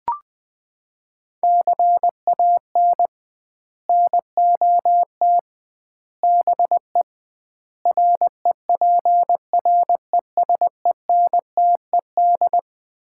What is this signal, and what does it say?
Morse code, heard as a keyed tone, about 20 wpm, 700 Hz. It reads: CAN NOT BE REPRESENTED
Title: 2x Word-Spacing - Sets of 5 Words - 20wpm